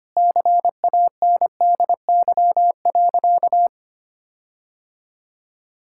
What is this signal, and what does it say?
Morse code, heard as a keyed tone, about 25 wpm, 700 Hz. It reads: CANDY.
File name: practice / 25wpm / Sentences with 4-Word Phrases - 25wpm